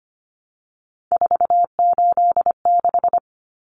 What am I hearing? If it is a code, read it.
Morse code, heard as a keyed tone, about 25 wpm, 700 Hz. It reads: 486
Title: Common Cut Numbers - 3-Digit 25wpm